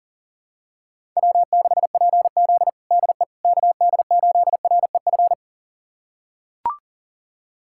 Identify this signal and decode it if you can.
Morse code, heard as a keyed tone, about 37 wpm, 700 Hz. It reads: W6PZ DE KD8REF
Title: USA Callsign DE Callsign 40wpm